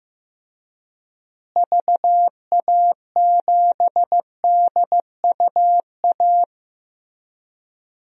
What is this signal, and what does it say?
Morse code, heard as a keyed tone, about 15 wpm, 700 Hz. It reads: VA7DUA
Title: DXCC-100 Call Signs - 6 Characters - Repeated 2x - 15wpm